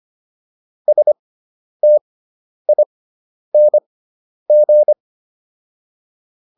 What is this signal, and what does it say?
Morse code, heard as a keyed tone, about 25 wpm, spaced as for 10 wpm, 600 Hz. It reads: STING
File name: ING5 10